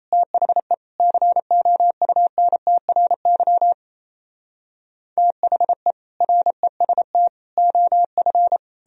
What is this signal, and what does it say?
Morse code, heard as a keyed tone, about 37 wpm, 700 Hz. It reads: THE COUNTRY THE REST OF